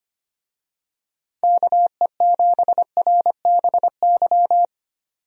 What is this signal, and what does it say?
Morse code, heard as a keyed tone, about 25 wpm, 700 Hz. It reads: KE7RBY
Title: USA Call Signs - 6 Characters - 25wpm